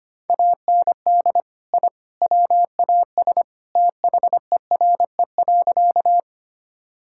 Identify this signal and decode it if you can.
Morse code, heard as a keyed tone, about 25 wpm, 700 Hz. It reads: AND I WAS THERE.
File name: practice / 25wpm / Sentences with 3-Word Phrases - 25wpm